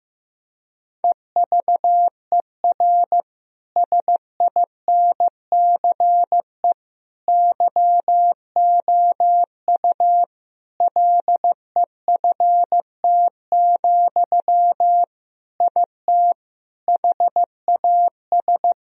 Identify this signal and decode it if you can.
Morse code, heard as a keyed tone, about 15 wpm, 700 Hz. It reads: EVER SINCE YOU LEFT, IT HAS
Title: Sentences from Top 1000 Words - Review Part 1 - 15wpm